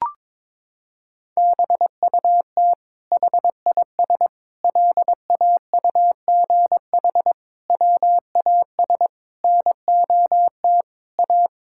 Morse code, heard as a keyed tone, about 22 wpm, 700 Hz. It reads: BUT HIS LAUGH WAS NOT A
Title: Sentences from Top 1000 Words - Review Part 5 - 22wpm